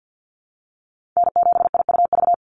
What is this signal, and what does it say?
Morse code, heard as a keyed tone, about 49 wpm, 700 Hz. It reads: N7IUV
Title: USA Call Signs 50wpm